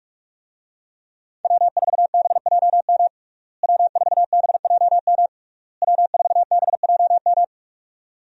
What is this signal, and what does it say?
Morse code, heard as a keyed tone, about 42 wpm, 700 Hz. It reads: W4BJM W4BJM W4BJM
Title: USA Call Signs - 5 Characters - Repeated 3x - 45wpm